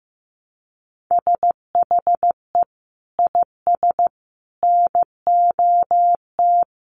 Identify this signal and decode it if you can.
Morse code, heard as a keyed tone, about 15 wpm, 700 Hz. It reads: SHE IS NOT
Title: Sentences from Top 700 Words 15wpm